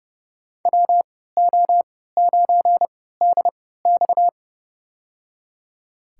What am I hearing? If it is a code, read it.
Morse code, heard as a keyed tone, about 30 wpm, spaced as for 17 wpm, 700 Hz. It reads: WO9DX
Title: USA Call Signs - 5 Characters - 3x Character Spacing - 30wpm